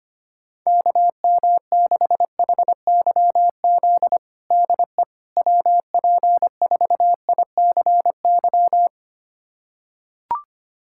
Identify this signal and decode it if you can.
Morse code, heard as a keyed tone, about 25 wpm, 700 Hz. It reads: KM6HYZ DE WP4ICY
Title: USA Callsign DE Callsign - 6 Characters - 25wpm